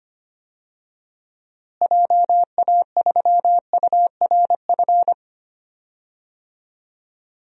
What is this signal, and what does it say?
Morse code, heard as a keyed tone, about 25 wpm, 700 Hz. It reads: JA3URF
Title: DXCC-100 Call Signs - 6 Characters - Repeated 2x - 25wpm